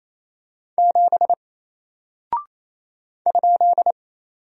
Morse code, heard as a keyed tone, about 28 wpm, 700 Hz. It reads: 7 ?
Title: Single Character (Letter:Number:Punctuation:Prosign) 28wpm